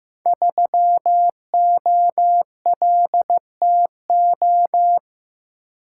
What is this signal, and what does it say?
Morse code, heard as a keyed tone, about 15 wpm, 700 Hz. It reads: 3OLTO
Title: Code Groups - 5 Letter:Numbers - 15wpm